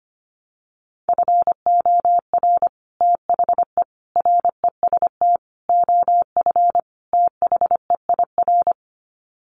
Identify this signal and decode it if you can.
Morse code, heard as a keyed tone, about 25 wpm, 700 Hz. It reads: FOR THE REST OF THEIR
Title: Sets of 5 Words Encore2 25wpm